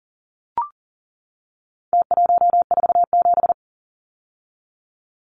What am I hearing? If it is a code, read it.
Morse code, heard as a keyed tone, about 40 wpm, 700 Hz. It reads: T147
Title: Numbers - Contest Style - 4-Digit 40wpm